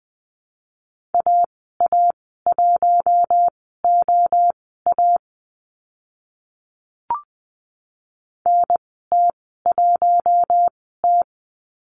Morse code, heard as a keyed tone, about 20 wpm, spaced as for 14 wpm, 700 Hz. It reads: AA1OA NT1T
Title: 042 Lesson 8g - Number 1 - Call Signs 20wpm